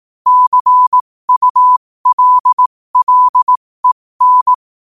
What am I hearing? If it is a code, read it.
Morse code, heard as a keyed tone, about 18 wpm, spaced as for 15 wpm, 1000 Hz. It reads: CULLEN
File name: 2024-11-18-15wpm